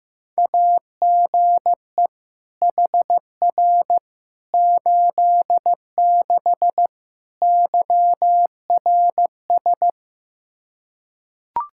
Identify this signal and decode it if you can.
Morse code, heard as a keyed tone, about 15 wpm, 700 Hz. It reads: AGE HR 86 YRS
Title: Practice QSO - Exchange 3 - More Info - 15wpm